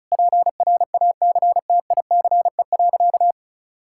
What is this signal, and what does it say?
Morse code, heard as a keyed tone, about 36 wpm, 700 Hz. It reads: PRACTICE.
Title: Sentences from Top 1000 Words - Review Part 6 - 35wpm